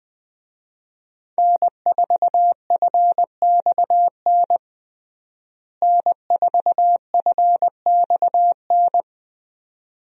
Morse code, heard as a keyed tone, about 21 wpm, 700 Hz. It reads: N4FXN N4FXN
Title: USA Call Signs - 5 Characters - Repeated 2x - 20wpm